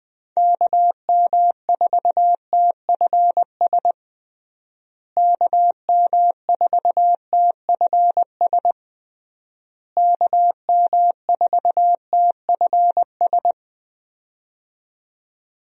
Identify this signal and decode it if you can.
Morse code, heard as a keyed tone, about 20 wpm, 700 Hz. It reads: KM4TFS KM4TFS KM4TFS
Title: USA Call Signs - 6 Characters - Repeated 3x - 20wpm